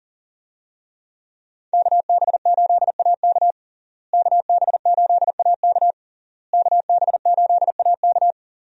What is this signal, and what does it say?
Morse code, heard as a keyed tone, about 37 wpm, 700 Hz. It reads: KB8AK KB8AK KB8AK
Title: USA Call Signs - 5 Characters - Repeated 3x - 40wpm